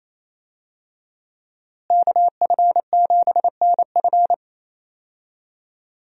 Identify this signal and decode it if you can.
Morse code, heard as a keyed tone, about 28 wpm, 700 Hz. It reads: KF7NF
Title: USA Call Signs - 5 Characters - Repeated 3x - 28wpm